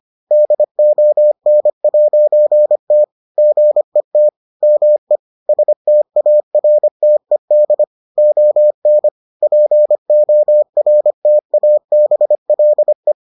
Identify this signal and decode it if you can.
Morse code, heard as a keyed tone, about 25 wpm, 600 Hz. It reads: DON'T GET ME STARTED ON PORTABLE